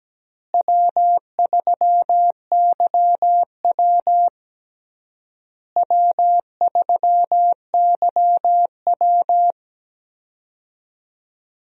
Morse code, heard as a keyed tone, about 17 wpm, 700 Hz. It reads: W3YW W3YW
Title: USA Call Signs - 4 Characters - Repeated 3x - 17wpm